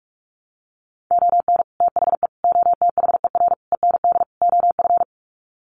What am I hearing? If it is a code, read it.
Morse code, heard as a keyed tone, about 44 wpm, 700 Hz. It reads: ON THE OTHER END OF
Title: Sets of 5 Words Encore 45wpm